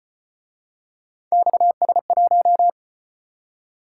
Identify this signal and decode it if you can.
Morse code, heard as a keyed tone, about 32 wpm, 700 Hz. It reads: XS1